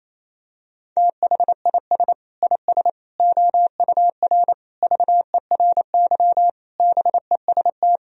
Morse code, heard as a keyed tone, about 28 wpm, 700 Hz. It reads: THIS IS OUR VERY BEST
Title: Sentences from Top 1000 Words - Review Part 4 - 28wpm